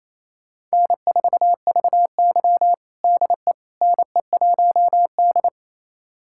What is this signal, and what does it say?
Morse code, heard as a keyed tone, about 28 wpm, 690 Hz. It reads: N4VY DE NE1D
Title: USA Callsign DE Callsign - 4 Characters - 28wpm